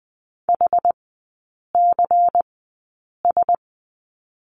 Morse code, heard as a keyed tone, about 20 wpm, spaced as for 8 wpm, 700 Hz. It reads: HCS
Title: 093 Lesson 16e - Letter c - Three Characters 20wpm